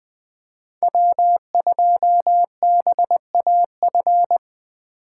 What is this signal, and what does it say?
Morse code, heard as a keyed tone, about 20 wpm, 700 Hz. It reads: W2BAF
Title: USA Call Signs - 5 Characters - 3x Character Spacing - 20wpm